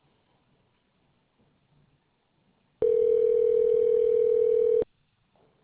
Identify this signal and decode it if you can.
Ringback tone (North America), 440 + 480 Hz.